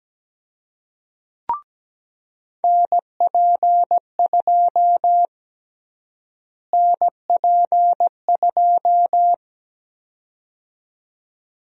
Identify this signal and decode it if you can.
Morse code, heard as a keyed tone, about 17 wpm, 700 Hz. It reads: NP2 NP2
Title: USA Call Sign Prefixes - Repeated 2x - 17wpm